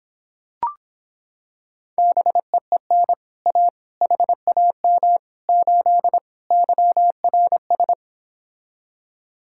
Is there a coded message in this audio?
Yes: Morse code from 2.0 to 7.9 s, heard as a keyed tone, about 26 wpm, 700 Hz. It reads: BEEN A HAM 8 YRS